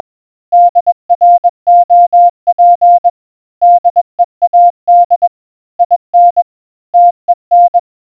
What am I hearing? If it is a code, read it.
Morse code, heard as a keyed tone, about 21 wpm, 700 Hz. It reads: DROP DEAD IN TEN